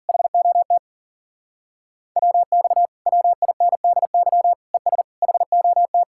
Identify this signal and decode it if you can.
Morse code, heard as a keyed tone, about 48 wpm, 700 Hz. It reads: HOT WX WINDY ES HOT
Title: Speed-Racing Common QSO Phrases Encore - 40wpm